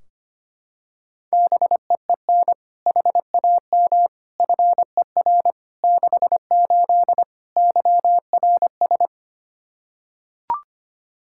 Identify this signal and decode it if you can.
Morse code, heard as a keyed tone, about 25 wpm, 700 Hz. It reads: BEEN HAM FER 68 YRS
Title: Practice QSO - Exchange 3 - More Info - 25wpm